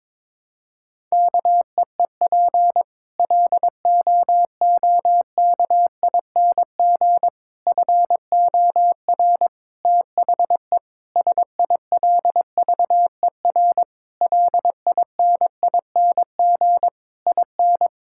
Morse code, heard as a keyed tone, about 22 wpm, 700 Hz. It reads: KEEP LOOKING FOR THE SILVER LINING IN
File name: Difficult Sentences with Hints - 22wpm